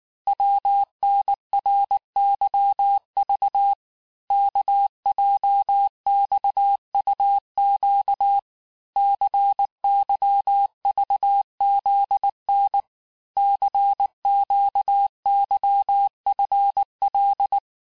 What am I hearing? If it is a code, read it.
Morse code, heard as a keyed tone, about 19 wpm, 800 Hz. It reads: WNRYV KJXUQ CYVZN CQYFL
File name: cours-du-lundi files / cours-1595-teste-CHIFFRE-vitesse18-MPM